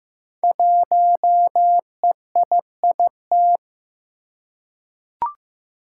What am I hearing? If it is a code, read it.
Morse code, heard as a keyed tone, about 15 wpm, 700 Hz. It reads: 1EIIT
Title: Five Challenging Letter-Number Combinations 15wpm